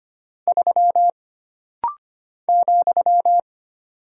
Morse code, heard as a keyed tone, about 25 wpm, 700 Hz. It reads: 3 ,
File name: Single Character (Letter:Number:Punctuation:Prosign) - Rapid-Fire - 25wpm